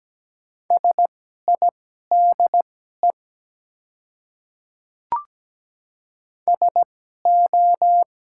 Morse code, heard as a keyed tone, about 17 wpm, spaced as for 12 wpm, 700 Hz. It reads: SIDE SO
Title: 2x Character Spacing - Top 100 Words - 17wpm